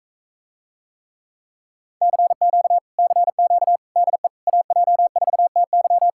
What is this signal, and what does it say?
Morse code, heard as a keyed tone, about 45 wpm, 700 Hz. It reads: CQ CQ DE AJ4TY